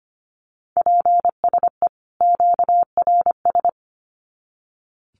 Morse code, heard as a keyed tone, about 25 wpm, 700 Hz. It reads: PSE QRS
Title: Common QSO Phrases 25wpm